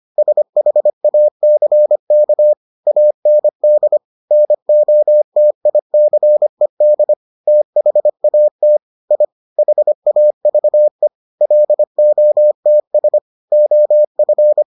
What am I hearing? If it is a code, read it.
Morse code, heard as a keyed tone, about 25 wpm, 600 Hz. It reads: SHACK AND NOTICED THAT I HAVE LOTS OF